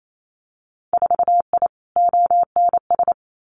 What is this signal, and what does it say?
Morse code, heard as a keyed tone, about 28 wpm, 700 Hz. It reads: 4I ONS
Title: Winter Field Day Exchanges Encore 28wpm